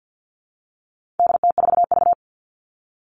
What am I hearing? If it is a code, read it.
Morse code, heard as a keyed tone, about 48 wpm, 700 Hz. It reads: DT4V
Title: All Cut Numbers - 4-Digit 50wpm